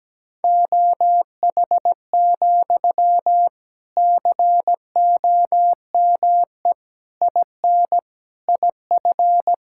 Morse code, heard as a keyed tone, about 17 wpm, 700 Hz. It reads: OH, COME IN IF
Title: Sentences from Top 1600 Words 17wpm